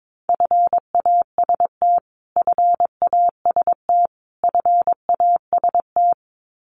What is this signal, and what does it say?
Morse code, heard as a keyed tone, about 22 wpm, 700 Hz. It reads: FAST FAST FAST
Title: Top 500 Words Repeated 22wpm